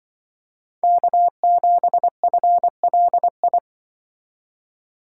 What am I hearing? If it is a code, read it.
Morse code, heard as a keyed tone, about 24 wpm, 700 Hz. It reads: K7FLI
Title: Speed Racing - USA 1x3 Call Signs 20wpm